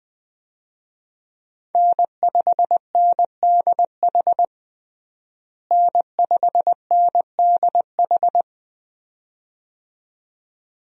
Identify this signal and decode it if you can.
Morse code, heard as a keyed tone, about 21 wpm, 700 Hz. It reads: N5NDH N5NDH
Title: USA Call Signs - 5 Characters - Repeated 2x - 20wpm